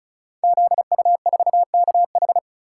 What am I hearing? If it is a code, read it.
Morse code, heard as a keyed tone, about 36 wpm, 700 Hz. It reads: ZU4KH
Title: Code Groups - 5 Letter:Numbers - 35wpm